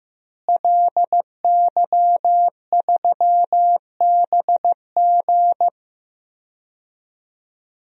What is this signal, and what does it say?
Morse code, heard as a keyed tone, about 15 wpm, 700 Hz. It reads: LY3BG
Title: DXCC-100 Call Signs - 5 Characters - Repeated 2x - 15wpm